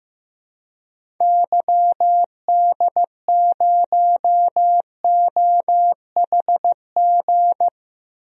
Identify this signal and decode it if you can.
Morse code, heard as a keyed tone, about 15 wpm, 700 Hz. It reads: YD0OHG
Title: DXCC-100 Call Signs - 6 Characters - Repeated 2x - 15wpm